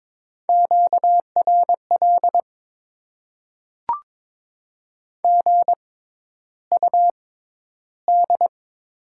Morse code, heard as a keyed tone, about 22 wpm, spaced as for 11 wpm, 700 Hz. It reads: QRL GUD
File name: Top 20 QSO Elements - 6x Character Spacing - 22wpm